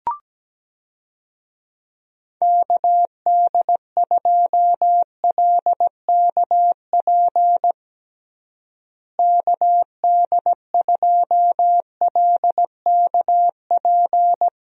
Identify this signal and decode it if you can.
Morse code, heard as a keyed tone, about 17 wpm, 700 Hz. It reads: KD2LKP KD2LKP
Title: USA Call Signs - 6 Characters - Repeated 3x - 17wpm